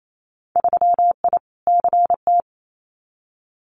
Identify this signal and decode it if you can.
Morse code, heard as a keyed tone, about 28 wpm, 700 Hz. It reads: 3I CT
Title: Winter Field Day Exchanges 28wpm